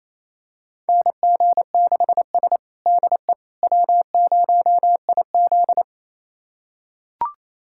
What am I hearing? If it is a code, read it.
Morse code, heard as a keyed tone, about 28 wpm, 700 Hz. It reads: NG6S DE W0IZ
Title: USA Callsign DE Callsign - 4 Characters - 28wpm